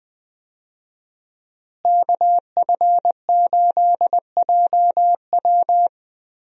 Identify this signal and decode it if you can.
Morse code, heard as a keyed tone, about 20 wpm, 700 Hz. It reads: KF8JW
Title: USA Call Signs - 5 Characters - Repeated 2x - 20wpm